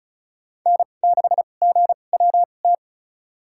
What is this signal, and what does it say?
Morse code, heard as a keyed tone, about 35 wpm, spaced as for 25 wpm, 700 Hz. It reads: N6GWT
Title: USA Call Signs - 5 Characters - 2x Character Spacing - 35wpm